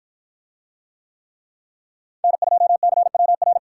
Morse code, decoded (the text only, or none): N2CWR